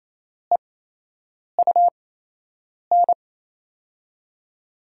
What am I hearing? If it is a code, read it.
Morse code, heard as a keyed tone, about 28 wpm, spaced as for 8 wpm, 700 Hz. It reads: EUN
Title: USA Call Signs - 5 Characters - 8x Character Spacing - 28wpm